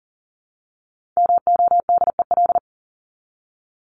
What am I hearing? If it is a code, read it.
Morse code, heard as a keyed tone, about 40 wpm, 700 Hz. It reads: MODEL
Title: Top 501-750 Words 40wpm